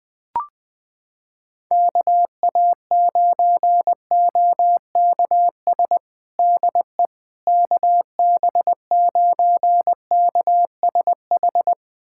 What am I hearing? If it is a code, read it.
Morse code, heard as a keyed tone, about 21 wpm, 700 Hz. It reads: KA9OKS DE KB9KSH
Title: USA Callsign DE Callsign - 6 Characters - 20wpm